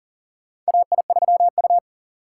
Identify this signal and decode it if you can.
Morse code, heard as a keyed tone, about 37 wpm, 700 Hz. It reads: AI3U
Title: USA Call Signs - 4 Characters - 3x Character Spacing - 40wpm